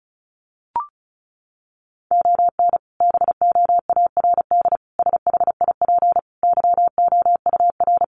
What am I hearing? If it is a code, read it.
Morse code, heard as a keyed tone, about 34 wpm, 700 Hz. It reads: ON BOARD SHIP YOUR
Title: Sentences from Top 1400 Words 35wpm